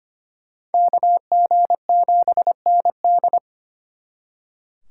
Morse code, heard as a keyed tone, about 32 wpm, 700 Hz. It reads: KG7ND